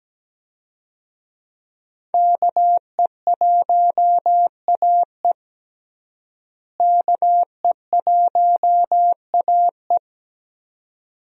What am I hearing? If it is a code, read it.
Morse code, heard as a keyed tone, about 17 wpm, 700 Hz. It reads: KE1AE KE1AE
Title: USA Call Signs - 5 Characters - Repeated 2x - 17wpm